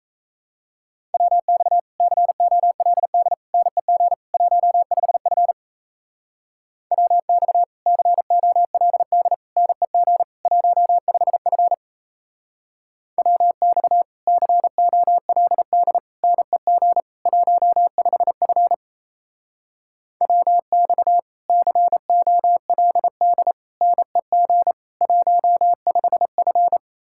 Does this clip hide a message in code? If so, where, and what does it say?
Yes: Morse code from 1.1 to 26.8 s, heard as a keyed tone, about 38 wpm, 700 Hz. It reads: WX COLD NEG 15F WX COLD NEG 15F WX COLD NEG 15F WX COLD NEG 15F